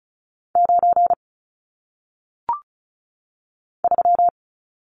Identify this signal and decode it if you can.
Morse code, heard as a keyed tone, about 36 wpm, 700 Hz. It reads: 9 3
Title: Numbers - 1-Digit 35wpm